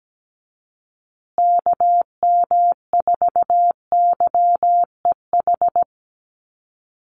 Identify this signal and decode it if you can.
Morse code, heard as a keyed tone, about 17 wpm, 700 Hz. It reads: KM4YEH
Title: USA Call Signs Repeated 2x 17wpm